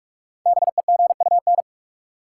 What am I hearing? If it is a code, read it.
Morse code, heard as a keyed tone, about 45 wpm, 700 Hz. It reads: BEGUN
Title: Sentences from Top 1000 Words - Review Part 6 - New Word Drill - 45wpm